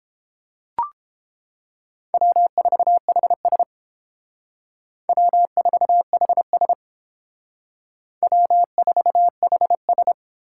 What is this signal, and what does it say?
Morse code, heard as a keyed tone, about 30 wpm, 700 Hz. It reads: W4HS W4HS W4HS